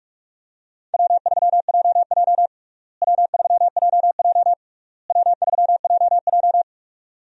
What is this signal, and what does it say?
Morse code, heard as a keyed tone, about 42 wpm, 700 Hz. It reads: W3JJ W3JJ W3JJ
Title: USA Call Signs - 4 Characters - Repeated 3x - 45wpm